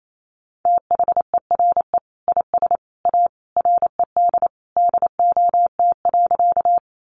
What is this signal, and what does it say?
Morse code, heard as a keyed tone, about 28 wpm, 700 Hz. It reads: THERE IS A RED DOT.
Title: Sentences from Top 1500 Words 28wpm